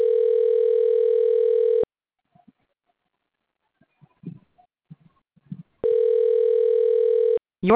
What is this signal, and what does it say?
Ringback tone (North America), 440 + 480 Hz.